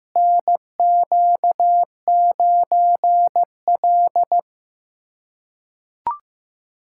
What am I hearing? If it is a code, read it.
Morse code, heard as a keyed tone, about 15 wpm, 700 Hz. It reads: NQ9L
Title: USA Callsign DE Callsign - 4 Characters - 15wpm